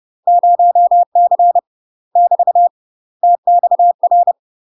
Morse code, heard as a keyed tone, about 30 wpm, 700 Hz. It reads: 0C <BT> TXR
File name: UK MASTER QSOs TRAINING / 30WPM / UK MASTER QSOs 0047